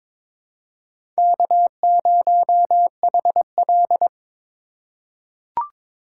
Morse code, heard as a keyed tone, about 21 wpm, 700 Hz. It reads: K0HL
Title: USA Call Signs - 4 Characters - 3x Character Spacing - 22wpm